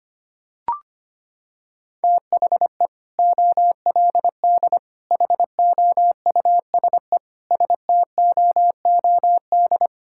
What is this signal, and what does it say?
Morse code, heard as a keyed tone, about 25 wpm, 690 Hz. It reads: THE OLD HOUSE STOOD
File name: practice / 25wpm / Sentences with 2-Word Phrases - 25wpm